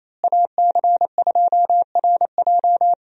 Morse code, heard as a keyed tone, about 27 wpm, 700 Hz. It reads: AC2RJ
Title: USA Call Signs - 5 Characters - Repeated 3x - 28wpm